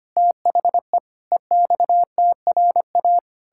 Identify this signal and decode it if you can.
Morse code, heard as a keyed tone, about 25 wpm, 700 Hz. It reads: THE EXTRA
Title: Pangram Challenge v2 - 25wpm